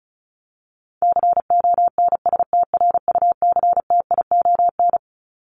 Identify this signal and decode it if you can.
Morse code, heard as a keyed tone, about 34 wpm, 700 Hz. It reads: CONSTRUCTION
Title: Top 1501-1750 Words 35wpm